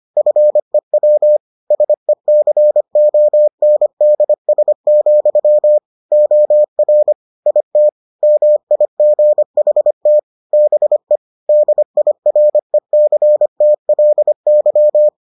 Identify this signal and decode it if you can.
Morse code, heard as a keyed tone, about 25 wpm, 600 Hz. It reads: FEW SECONDS, OR IT MIGHT BE DIRECTLY